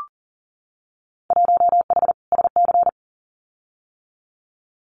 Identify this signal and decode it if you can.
Morse code, heard as a keyed tone, about 40 wpm, 700 Hz. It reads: 1H SC
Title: Winter Field Day Exchanges 40wpm